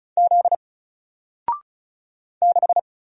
Morse code, heard as a keyed tone, about 35 wpm, 700 Hz. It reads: Z 6
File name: Single-Letter-Number-35wpm